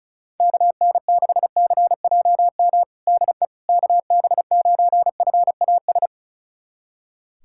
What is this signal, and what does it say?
Morse code, heard as a keyed tone, about 36 wpm, 700 Hz. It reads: KN6CJM DE KB9FAS
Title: USA Callsign DE Callsign 35wpm